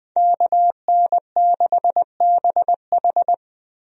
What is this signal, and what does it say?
Morse code, heard as a keyed tone, about 20 wpm, 700 Hz. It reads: KN6BH
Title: USA Call Signs - 5 Characters - Repeated 2x - 20wpm